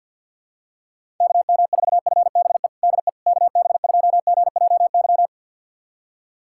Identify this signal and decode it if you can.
Morse code, heard as a keyed tone, about 50 wpm, 700 Hz. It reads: KM4FBE DE KB2GJY
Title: USA Callsign DE Callsign - 6 Characters - 50wpm